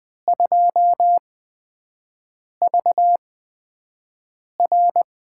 Morse code, heard as a keyed tone, about 20 wpm, spaced as for 5 wpm, 700 Hz. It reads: 2VR
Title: USA Call Signs - 4 Characters - 8x Character Spacing - 20wpm